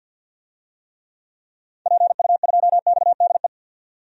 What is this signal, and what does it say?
Morse code, heard as a keyed tone, about 52 wpm, 700 Hz. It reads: PU2XDE
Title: DXCC-100 Call Signs - 6 Characters - 50wpm